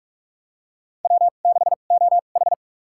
Morse code, heard as a keyed tone, about 46 wpm, spaced as for 32 wpm, 700 Hz. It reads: W6OH
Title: USA Call Signs - 4 Characters - 2x Character Spacing - 45wpm